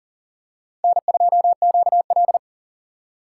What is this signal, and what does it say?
Morse code, heard as a keyed tone, about 40 wpm, 700 Hz. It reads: N2QL
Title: USA Call Signs - 4 Characters - 3x Character Spacing - 40wpm